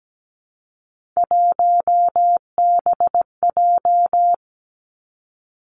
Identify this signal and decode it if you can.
Morse code, heard as a keyed tone, about 17 wpm, 700 Hz. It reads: 1BJ
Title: Three Challenging Letter-Number Combinations 17wpm